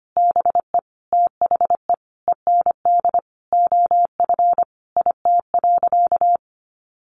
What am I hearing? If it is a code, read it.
Morse code, heard as a keyed tone, about 25 wpm, 700 Hz. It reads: BE THE END OF IT.
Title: Sentences from Top 1200 Words 25wpm